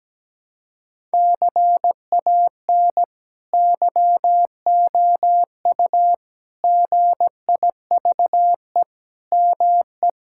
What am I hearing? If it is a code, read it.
Morse code, heard as a keyed tone, about 17 wpm, 700 Hz. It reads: CAN YOU GIVE ME
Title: Sentences from Top 1000 Words - Review Part 6 - 17wpm